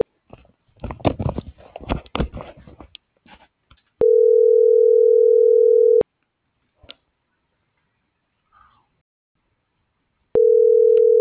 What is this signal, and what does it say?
Ringback tone (North America), 440 + 480 Hz.